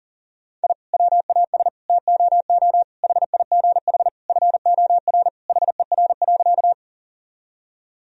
Morse code, heard as a keyed tone, about 40 wpm, 700 Hz. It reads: I WAS TOO HIGH FOR HER.
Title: Sentences from Top 1000 Words - Review Part 1 - 40wpm